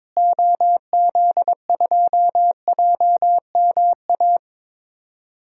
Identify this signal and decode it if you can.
Morse code, heard as a keyed tone, about 22 wpm, 700 Hz. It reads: OZ2JMA
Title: DXCC-100 Call Signs - 6 Characters - Repeated 3x - 22wpm